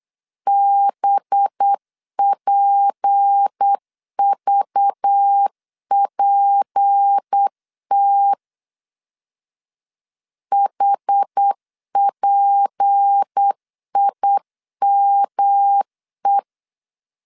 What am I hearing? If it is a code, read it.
Morse code, heard as a keyed tone, about 8 wpm, 800 Hz. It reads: BPVPT HPIME
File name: LEZIONI / 6 PKX0 / 6 P 8wpm 3025